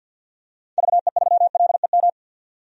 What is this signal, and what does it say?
Morse code, heard as a keyed tone, about 53 wpm, 700 Hz. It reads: VE3LEM
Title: DXCC-100 Call Signs - 6 Characters - 50wpm